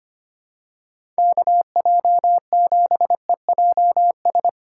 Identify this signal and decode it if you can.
Morse code, heard as a keyed tone, about 25 wpm, 700 Hz. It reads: KJ7EJS
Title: USA Call Signs - 6 Characters - 3x Character Spacing - 25wpm